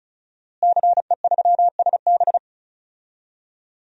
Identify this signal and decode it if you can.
Morse code, heard as a keyed tone, about 34 wpm, 700 Hz. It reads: CE3SB
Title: DXCC-100 Call Signs - 5 Characters - Repeated 2x - 35wpm